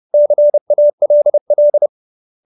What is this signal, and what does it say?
Morse code, heard as a keyed tone, about 30 wpm, 590 Hz. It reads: CALL